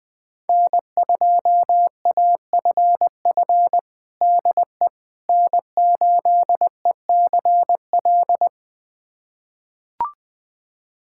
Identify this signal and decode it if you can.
Morse code, heard as a keyed tone, about 20 wpm, 700 Hz. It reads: N2AFF DE N8ECL
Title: USA Callsign DE Callsign - 5 Characters - 20wpm